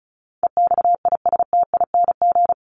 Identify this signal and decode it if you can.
Morse code, heard as a keyed tone, about 34 wpm, 700 Hz. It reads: EXISTING
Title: ING Suffix - Top 100 Words with Base Words 35wpm